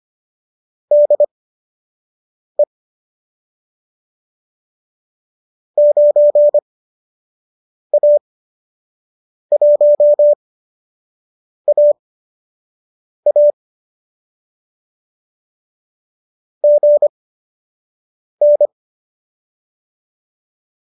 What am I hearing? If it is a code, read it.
Morse code, heard as a keyed tone, about 25 wpm, spaced as for 6 wpm, 590 Hz. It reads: DE 9A1AA GN
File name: qso012 06